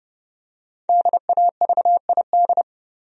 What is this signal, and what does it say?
Morse code, heard as a keyed tone, about 30 wpm, 700 Hz. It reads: DAVID
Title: USA Common Names - Max 5 Letters - 30wpm